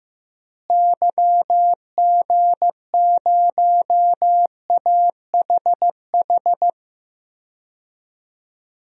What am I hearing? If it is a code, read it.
Morse code, heard as a keyed tone, about 15 wpm, 700 Hz. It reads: YG0AHH
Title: DXCC-100 Call Signs - 6 Characters - Repeated 3x - 15wpm